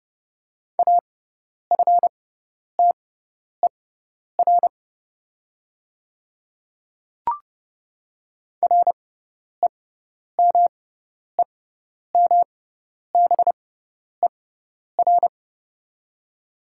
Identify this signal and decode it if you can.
Morse code, heard as a keyed tone, about 30 wpm, spaced as for 10 wpm, 700 Hz. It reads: AFTER REMEMBER
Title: 6x Character Spacing - Top 500 Words - 30wpm